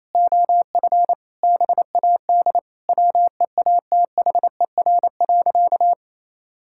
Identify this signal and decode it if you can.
Morse code, heard as a keyed tone, about 28 wpm, 700 Hz. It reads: OF BAD WEATHER.
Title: Sentences from Top 1000 Words - Review Part 6 - 28wpm